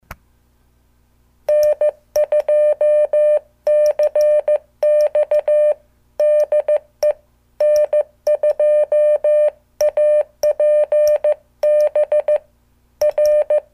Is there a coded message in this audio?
Yes: Morse code from 1.5 to 13.6 s, heard as a keyed tone, about 15 wpm, 600 Hz. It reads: N2CX DE N2APB R